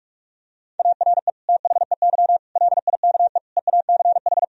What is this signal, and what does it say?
Morse code, heard as a keyed tone, about 41 wpm, 700 Hz. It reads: ARE THEY LIKE EACH
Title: Sentences from Top 1000 Words - Review Part 1 - 45wpm